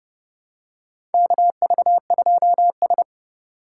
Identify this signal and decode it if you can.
Morse code, heard as a keyed tone, about 32 wpm, 700 Hz. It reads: KV2S
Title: USA Call Signs - 4 Characters - 4x Character Spacing - 30wpm